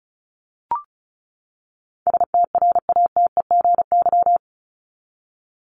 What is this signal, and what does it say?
Morse code, heard as a keyed tone, about 34 wpm, 700 Hz. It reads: STRATEGY
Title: Top 501-750 Words 35wpm